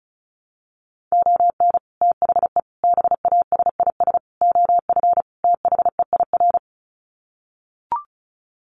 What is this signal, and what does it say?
Morse code, heard as a keyed tone, about 34 wpm, 700 Hz. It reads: ON THE BASIS OF THEIR
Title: Sets of 5 Words Encore 35wpm